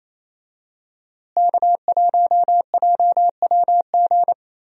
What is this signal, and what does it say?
Morse code, heard as a keyed tone, about 28 wpm, 700 Hz. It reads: K1JWG
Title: USA Call Signs - 5 Characters - Repeated 3x - 28wpm